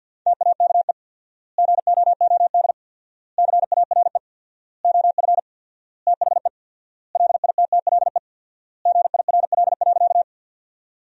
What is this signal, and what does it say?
Morse code, heard as a keyed tone, about 50 wpm, 700 Hz. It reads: TAKE GOOD CARE OF THE LITTLE GIRL.
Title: 4x Word-Spacing - Sentences from Top 400 Words - 50wpm